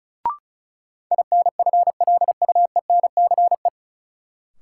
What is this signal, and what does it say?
Morse code, heard as a keyed tone, about 36 wpm, 700 Hz. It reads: INFLUENCE
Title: Sentences from Top 1100 Words - New Word Drill Alt - 35wpm